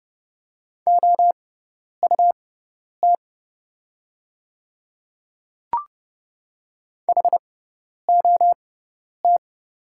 Morse code, heard as a keyed tone, about 31 wpm, spaced as for 10 wpm, 700 Hz. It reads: OUT HOT
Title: 6x Character Spacing - Top 100 Words - 30wpm